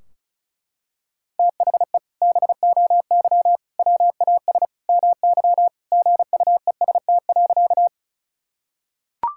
Morse code, heard as a keyed tone, about 36 wpm, 700 Hz. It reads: THE BOY WAS MY GUEST.
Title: Sentences from Top 1600 Words 35wpm